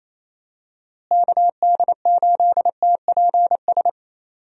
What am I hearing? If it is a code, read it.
Morse code, heard as a keyed tone, about 28 wpm, 700 Hz. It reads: KD8TPS
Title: USA Call Signs - 6 Characters - Repeated 2x - 28wpm